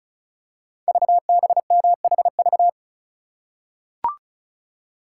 Morse code, heard as a keyed tone, about 36 wpm, 700 Hz. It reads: VBMHV
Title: Code Groups - 5 Letters - 35wpm